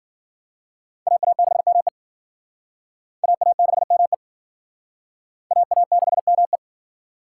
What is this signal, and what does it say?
Morse code, heard as a keyed tone, about 52 wpm, 700 Hz. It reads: AA6ME AA6ME AA6ME